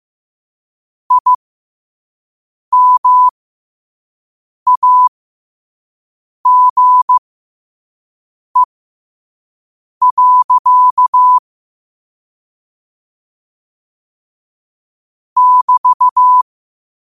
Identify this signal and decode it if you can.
Morse code, heard as a keyed tone, about 15 wpm, spaced as for 5 wpm, 1000 Hz. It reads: IMAGE. <BT>
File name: qotdfiles / 5wpm